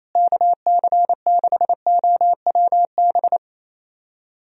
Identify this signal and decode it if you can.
Morse code, heard as a keyed tone, about 27 wpm, 700 Hz. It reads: KC6OWB
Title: USA Call Signs - 6 Characters - Repeated 3x - 28wpm